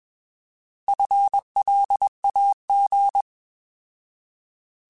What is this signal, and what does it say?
Morse code, heard as a keyed tone, about 21 wpm, 800 Hz. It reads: FLAG